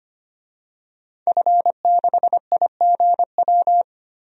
Morse code, heard as a keyed tone, about 25 wpm, 700 Hz. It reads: F6IGW
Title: DXCC-100 Call Signs - 5 Characters - 25wpm